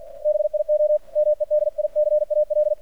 Morse code, heard as a keyed tone, about 48 wpm, 600 Hz. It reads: BIO MEDICAL
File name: archived / CRSnet